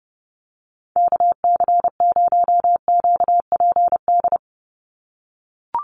Morse code, heard as a keyed tone, about 30 wpm, 700 Hz. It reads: KC0QPD
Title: USA Call Signs 30wpm